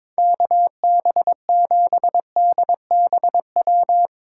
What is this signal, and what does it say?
Morse code, heard as a keyed tone, about 22 wpm, 700 Hz. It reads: KB7DBW
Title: USA Call Signs - 6 Characters - 8x Character Spacing - 22wpm